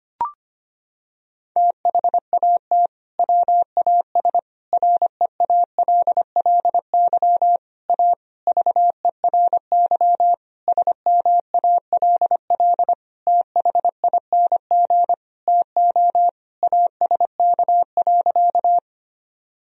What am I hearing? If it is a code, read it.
Morse code, heard as a keyed tone, about 25 wpm, 700 Hz. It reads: THAT WAS REALLY A VERY SMALL THING TO ASK.
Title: Sentences from Top 1000 Words - Review Part 1 - 25wpm